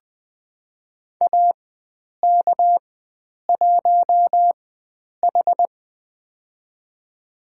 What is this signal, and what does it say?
Morse code, heard as a keyed tone, about 20 wpm, spaced as for 9 wpm, 700 Hz. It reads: AK1H
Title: USA Call Signs - 4 Characters - 4x Character Spacing - 20wpm